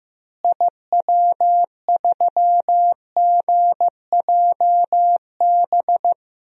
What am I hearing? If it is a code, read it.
Morse code, heard as a keyed tone, about 15 wpm, 700 Hz. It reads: IW3GJB
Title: DXCC-100 Call Signs - 6 Characters - Repeated 2x - 15wpm